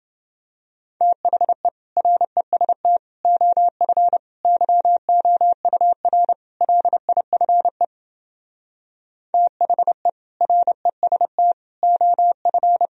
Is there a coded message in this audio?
Yes: Morse code from 1.0 to 12.9 s, heard as a keyed tone, about 31 wpm, 700 Hz. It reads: THE REST OF YOUR LIFE THE REST OF